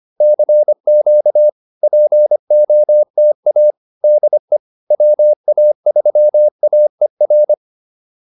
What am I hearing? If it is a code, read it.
Morse code, heard as a keyed tone, about 25 wpm, 600 Hz. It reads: CQ POTA DE WA3AER